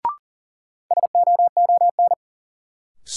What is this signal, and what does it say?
Morse code, heard as a keyed tone, about 40 wpm, 700 Hz. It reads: SOON
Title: Sets of 3 Words - New Word Drill - 40wpm